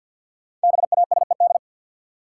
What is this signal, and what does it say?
Morse code, heard as a keyed tone, about 53 wpm, 700 Hz. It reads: BASED
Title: Sets of 2 Words - New Word Drill - 50wpm